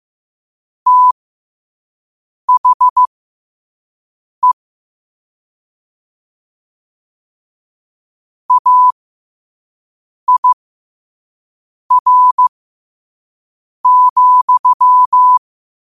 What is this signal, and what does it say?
Morse code, heard as a keyed tone, about 15 wpm, spaced as for 5 wpm, 1000 Hz. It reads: THE AIR,
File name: qotdfiles / 5wpm